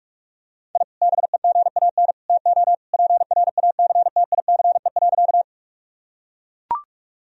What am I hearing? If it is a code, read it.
Morse code, heard as a keyed tone, about 41 wpm, 700 Hz. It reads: I BEGAN TO PRACTICE.
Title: Sentences from Top 1000 Words - Review Part 4 - 45wpm